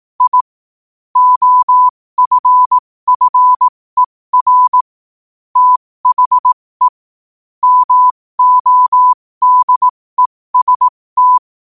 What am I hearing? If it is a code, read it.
Morse code, heard as a keyed tone, about 18 wpm, spaced as for 15 wpm, 1000 Hz. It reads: I OFFER THE MODEST
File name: qotdfiles / 15wpm